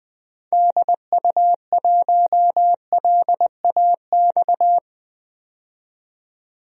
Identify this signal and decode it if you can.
Morse code, heard as a keyed tone, about 20 wpm, 700 Hz. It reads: DU1LAX
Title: DXCC-100 Call Signs - 6 Characters - 20wpm